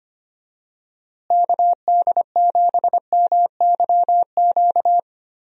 Morse code, heard as a keyed tone, about 25 wpm, 700 Hz. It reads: KD7MYQ
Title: USA Call Signs - 6 Characters - Repeated 3x - 25wpm